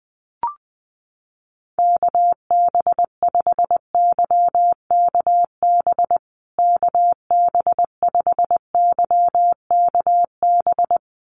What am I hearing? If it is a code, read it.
Morse code, heard as a keyed tone, about 20 wpm, 700 Hz. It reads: KB5YKB KB5YKB
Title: USA Call Signs Repeated 2x 20wpm